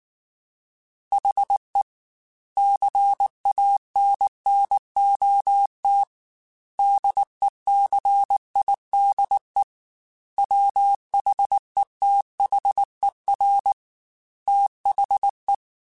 Morse code, heard as a keyed tone, about 19 wpm, 800 Hz. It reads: HE CANNOT DECIDE WHETHER THE